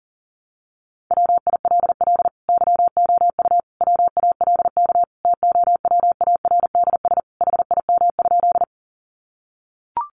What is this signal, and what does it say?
Morse code, heard as a keyed tone, about 40 wpm, 700 Hz. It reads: WILL YOU WALK TOWARDS HIM?
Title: Sentences from Top 1100 Words 40wpm